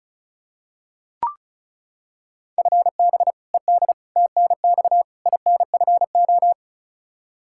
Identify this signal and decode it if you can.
Morse code, heard as a keyed tone, about 34 wpm, 700 Hz. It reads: FB ED TNX INFO
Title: Practice QSO - Exchange 3 - More Info - 35wpm